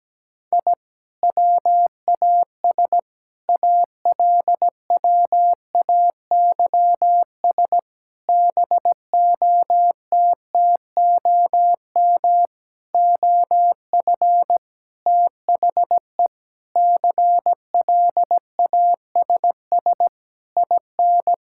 Morse code, heard as a keyed tone, about 17 wpm, 700 Hz. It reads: I WAS ALWAYS BOTTOM OF THE CLASS IN
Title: Sentences from Top 1000 Words - Review Part 6 - 17wpm